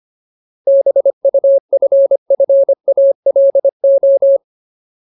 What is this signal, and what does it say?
Morse code, heard as a keyed tone, about 25 wpm, 550 Hz. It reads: BUFFALO
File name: Ultimate Word List / 25WPM Ultimate